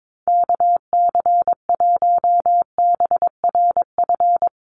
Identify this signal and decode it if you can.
Morse code, heard as a keyed tone, about 22 wpm, 700 Hz. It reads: KC1BRF
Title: USA Call Signs Repeated 2x 22wpm